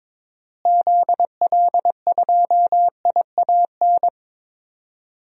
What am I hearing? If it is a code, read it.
Morse code, heard as a keyed tone, about 22 wpm, 700 Hz. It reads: ZL2IAN
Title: DXCC-100 Call Signs - 6 Characters - Repeated 3x - 22wpm